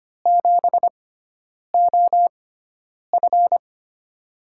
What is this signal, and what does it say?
Morse code, heard as a keyed tone, about 25 wpm, spaced as for 9 wpm, 700 Hz. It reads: 7OF
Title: USA Call Signs - 4 Characters - 6x Character Spacing - 25wpm